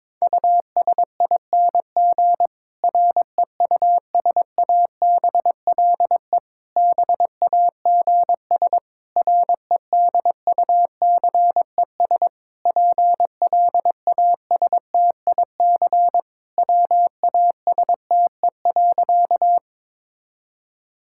Morse code, heard as a keyed tone, about 22 wpm, 700 Hz. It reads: USING REUSABLE BAGS REDUCES PLASTIC WASTE.
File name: Difficult Sentences with Hints - 22wpm